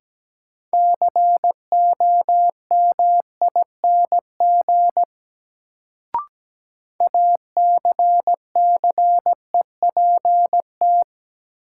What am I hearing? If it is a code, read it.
Morse code, heard as a keyed tone, about 17 wpm, 700 Hz. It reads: COMING ACCEPT
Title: Sentences from Top 1300 Words - New Word Drill - 17wpm